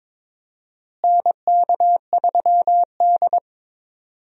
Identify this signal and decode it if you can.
Morse code, heard as a keyed tone, about 22 wpm, 700 Hz. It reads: NK3D
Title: USA Call Signs - 4 Characters - Repeated 2x - 22wpm